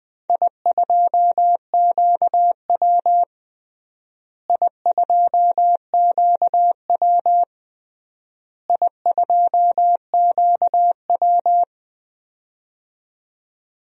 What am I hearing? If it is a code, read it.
Morse code, heard as a keyed tone, about 20 wpm, 700 Hz. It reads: I2QW I2QW I2QW
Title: DXCC-100 Call Signs - 4 Characters - Repeated 3x - 20wpm